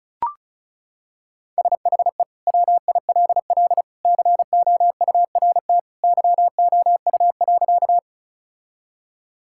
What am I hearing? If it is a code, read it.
Morse code, heard as a keyed tone, about 36 wpm, 700 Hz. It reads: SHE WILL COURT YOU.
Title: Sentences from Top 1600 Words 35wpm